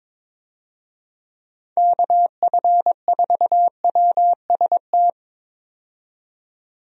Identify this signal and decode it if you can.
Morse code, heard as a keyed tone, about 22 wpm, 700 Hz. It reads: KF4WST
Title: USA Call Signs - 6 Characters - 22wpm